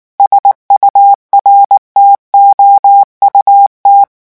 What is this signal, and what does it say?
Morse code, heard as a keyed tone, about 19 wpm, 800 Hz. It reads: SURTOUT
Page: REVISION 8 - 8 mots minute